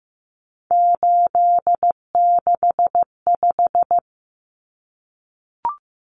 Morse code, heard as a keyed tone, about 15 wpm, 700 Hz. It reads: 865
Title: Common Cut Numbers - 3-Digit 15wpm